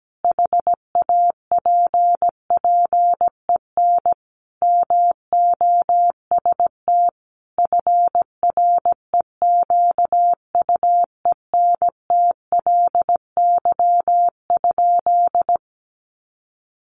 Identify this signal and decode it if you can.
Morse code, heard as a keyed tone, about 17 wpm, 700 Hz. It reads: HAPPEN MOST FREQUENTLY?
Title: Sentences from Top 1300 Words 17wpm